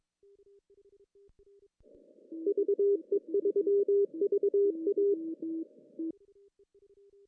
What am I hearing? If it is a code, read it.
Morse code, heard as a keyed tone, about 22 wpm, 400 Hz. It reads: VE3VA
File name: CW / cqww-cw-k5tr